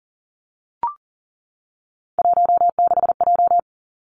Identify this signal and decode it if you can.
Morse code, heard as a keyed tone, about 40 wpm, 700 Hz. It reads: 16J
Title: Three Challenging Letter-Number Combinations 40wpm